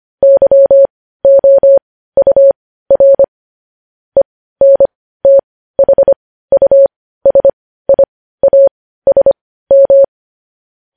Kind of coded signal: Morse code